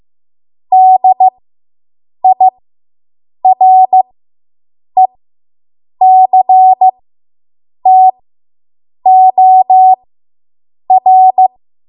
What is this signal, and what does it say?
Morse code, heard as a keyed tone, about 15 wpm, spaced as for 7 wpm, 750 Hz. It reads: DIRECTOR